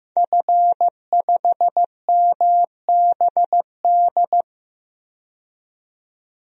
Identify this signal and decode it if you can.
Morse code, heard as a keyed tone, about 16 wpm, 700 Hz. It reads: F5MBD
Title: DXCC-100 Call Signs - 5 Characters - 15wpm